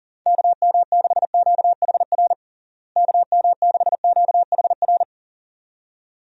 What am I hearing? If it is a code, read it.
Morse code, heard as a keyed tone, about 40 wpm, 700 Hz. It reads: KM6QHR KM6QHR
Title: USA Call Signs - 6 Characters - Repeated 2x - 40wpm